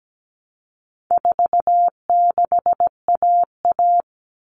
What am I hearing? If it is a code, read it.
Morse code, heard as a keyed tone, about 17 wpm, 700 Hz. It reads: 46AA
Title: All Cut Numbers - 4-Digit 17wpm